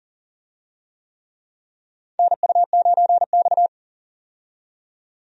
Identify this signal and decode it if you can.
Morse code, heard as a keyed tone, about 40 wpm, 700 Hz. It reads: NU9X
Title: USA Call Signs - 4 Characters - 40wpm